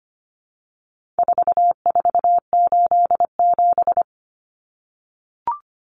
Morse code, heard as a keyed tone, about 25 wpm, 700 Hz. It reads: 4487
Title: Numbers - 4-Digit 25wpm